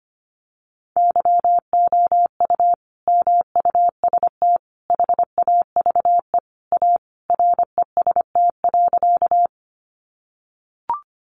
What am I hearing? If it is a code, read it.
Morse code, heard as a keyed tone, about 25 wpm, 700 Hz. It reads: YOU MUST HAVE A REST.
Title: Sentences from Top 500 Words 25wpm